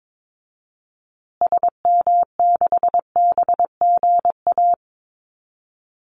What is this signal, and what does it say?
Morse code, heard as a keyed tone, about 22 wpm, 700 Hz. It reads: SM6BGA
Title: Call Signs Repeated 2x - 22wpm